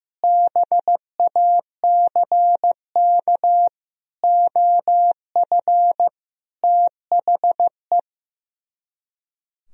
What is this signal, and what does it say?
Morse code, heard as a keyed tone, about 16 wpm, 700 Hz. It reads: BACK OF THE
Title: Speed-Racing - Sets of 5 Words 15wpm